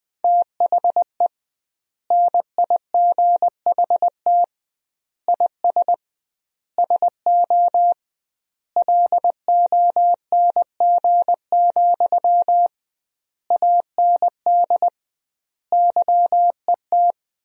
Morse code, heard as a keyed tone, about 20 wpm, 700 Hz. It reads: THE NIGHT IS SO LONG, AND YET
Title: 2x Word-Spacing - Sentences from Top 500 Words - 20wpm